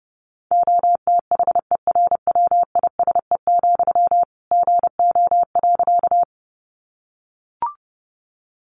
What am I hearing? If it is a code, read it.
Morse code, heard as a keyed tone, about 30 wpm, 700 Hz. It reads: OTHERWISE, GO.
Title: Sentences from Top 1200 Words 30wpm